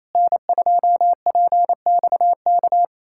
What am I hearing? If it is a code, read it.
Morse code, heard as a keyed tone, about 27 wpm, 700 Hz. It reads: N2PXK
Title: USA Callsign DE Callsign - 5 Characters - 28wpm